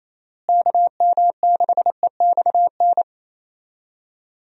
Morse code, heard as a keyed tone, about 27 wpm, 700 Hz. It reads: KM6EXN
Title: USA Call Signs - 6 Characters - Repeated 2x - 28wpm